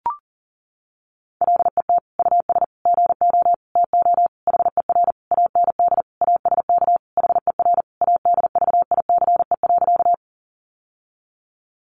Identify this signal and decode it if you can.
Morse code, heard as a keyed tone, about 40 wpm, 700 Hz. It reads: LET US GO TO HER AND ASK HER ADVICE.
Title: Sentences from Top 1100 Words 40wpm